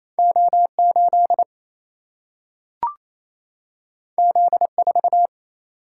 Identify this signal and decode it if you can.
Morse code, heard as a keyed tone, about 28 wpm, 700 Hz. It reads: O 8 Z 4
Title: Single-Letter Single-Number 28wpm